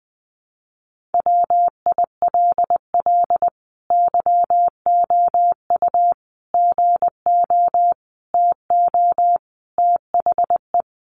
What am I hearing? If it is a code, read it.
Morse code, heard as a keyed tone, about 19 wpm, 700 Hz. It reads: WILL YOU GO TO THE
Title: Sentences from Top 1500 Words 20wpm